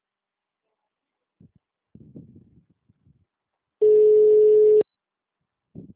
Ringback tone (Europe, 425 Hz).